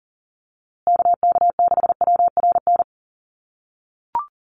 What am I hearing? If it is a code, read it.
Morse code, heard as a keyed tone, about 40 wpm, 700 Hz. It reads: KK6WRN
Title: USA Call Signs Repeated 2x 40wpm